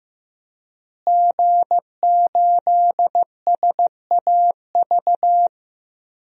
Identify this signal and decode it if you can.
Morse code, heard as a keyed tone, about 15 wpm, 700 Hz. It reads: G8SAV
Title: DXCC-100 Call Signs - 5 Characters - Repeated 2x - 15wpm